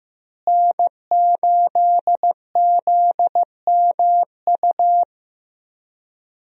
Morse code, heard as a keyed tone, about 15 wpm, 700 Hz. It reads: N8ZMU
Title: USA Call Signs - 5 Characters - Repeated 3x - 15wpm